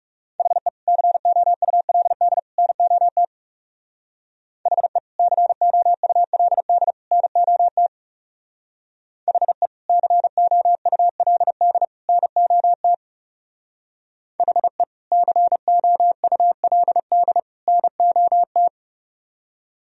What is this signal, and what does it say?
Morse code, heard as a keyed tone, about 35 wpm, 700 Hz. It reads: HE COULD NOT HE COULD NOT HE COULD NOT HE COULD NOT
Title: Speed-Racing - Sets of 3 Words 30wpm